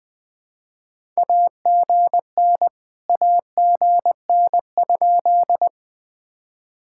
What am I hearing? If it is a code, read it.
Morse code, heard as a keyed tone, about 21 wpm, 700 Hz. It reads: AGN AGN?
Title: POTA QSO Practice - 20wpm